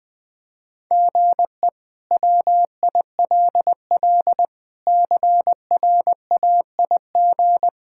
Morse code, heard as a keyed tone, about 20 wpm, 700 Hz. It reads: GE WILL CRAIG
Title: K1USN SST QSO Practice - 20wpm